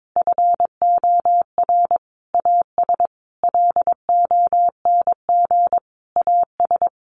Morse code, heard as a keyed tone, about 22 wpm, 700 Hz. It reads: FOR AS LONG AS
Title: Sentences from Top 900 Words 22wpm